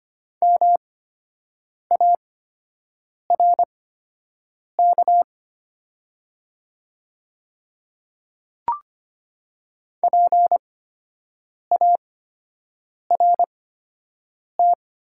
Morse code, heard as a keyed tone, about 25 wpm, spaced as for 7 wpm, 700 Hz. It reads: MARK PART
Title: 8x Character Spacing - Top 300 Words - 25wpm